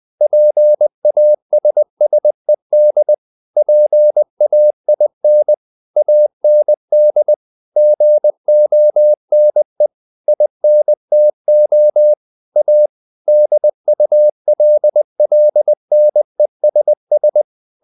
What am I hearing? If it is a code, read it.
Morse code, heard as a keyed tone, about 20 wpm, 600 Hz. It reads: PASSED PAIN AND GONE INTO A DULLNESS
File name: the old man and the sea part 2 20wpm